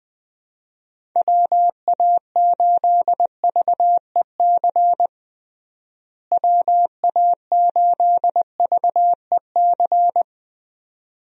Morse code, heard as a keyed tone, about 20 wpm, 700 Hz. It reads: WA8VEC WA8VEC
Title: USA Call Signs - 6 Characters - Repeated 2x - 20wpm